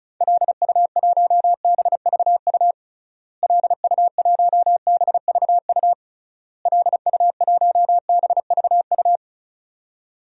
Morse code, heard as a keyed tone, about 36 wpm, 700 Hz. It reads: LU1BVU LU1BVU LU1BVU
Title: DXCC-100 Call Signs - 6 Characters - Repeated 3x - 35wpm